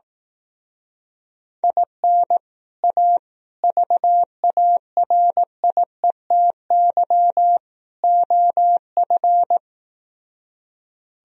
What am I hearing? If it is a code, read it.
Morse code, heard as a keyed tone, about 18 wpm, 700 Hz. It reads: IN A VARIETY OF
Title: Speed-Racing - Sets of 4 Words 15wpm